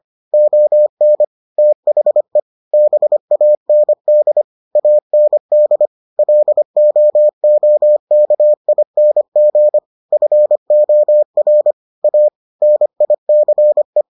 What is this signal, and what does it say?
Morse code, heard as a keyed tone, about 25 wpm, 600 Hz. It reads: ON THE BAND AND LOOKING FOR A NICE